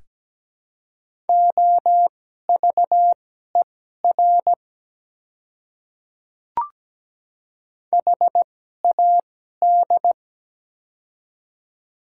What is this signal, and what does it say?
Morse code, heard as a keyed tone, about 17 wpm, spaced as for 12 wpm, 700 Hz. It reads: OVER HAD
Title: 2x Character Spacing - Top 100 Words - 17wpm